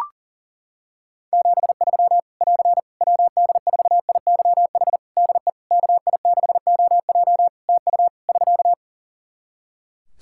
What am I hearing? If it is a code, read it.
Morse code, heard as a keyed tone, about 40 wpm, 700 Hz. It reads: 73 <AR> WD4IYH DE KI6OJ TU <SK>
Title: K1USN SST QSO Practice 40wpm